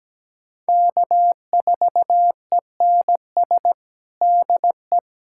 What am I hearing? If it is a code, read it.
Morse code, heard as a keyed tone, about 17 wpm, 700 Hz. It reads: K4ENS DE
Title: USA Callsign DE Callsign 17wpm